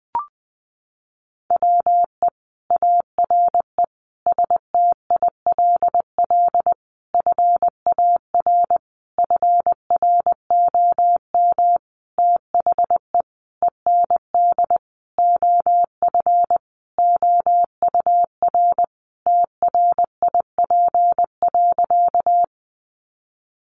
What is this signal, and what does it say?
Morse code, heard as a keyed tone, about 20 wpm, 700 Hz. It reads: WE ARE STILL FAR FROM THE END OF OUR TRIP.
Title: Sentences from Top 1300 Words 20wpm